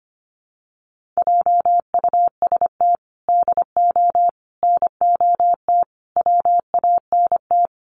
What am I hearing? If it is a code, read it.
Morse code, heard as a keyed tone, about 25 wpm, 700 Hz. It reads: JUST DO NOT WANT
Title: Sets of 5 Words Encore 25wpm